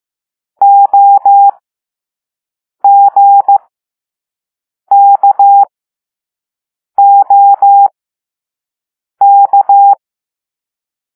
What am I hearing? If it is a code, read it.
Morse code, heard as a keyed tone, about 15 wpm, spaced as for 5 wpm, 800 Hz. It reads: OGKOK
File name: Morse / Level4B